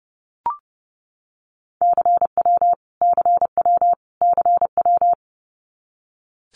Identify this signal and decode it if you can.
Morse code, heard as a keyed tone, about 30 wpm, 700 Hz. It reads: CW CW CW
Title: Abbreviations Repeated 3x - 30wpm